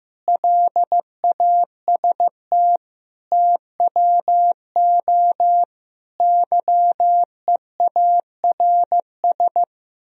Morse code, heard as a keyed tone, about 15 wpm, 700 Hz. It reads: LAST TWO YEARS
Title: Speed-Racing - Sets of 5 Words 15wpm